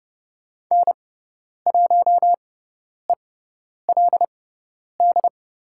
Morse code, heard as a keyed tone, about 30 wpm, spaced as for 10 wpm, 700 Hz. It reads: N1ELD
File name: Lesson 16g - Letter c - Call Signs 30wpm